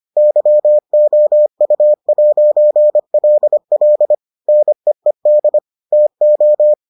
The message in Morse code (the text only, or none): YOU'LL NEED TO